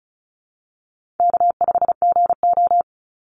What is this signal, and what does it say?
Morse code, heard as a keyed tone, about 34 wpm, 700 Hz. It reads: K5GO
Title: Easy Call Signs Repeated 3x - 35wpm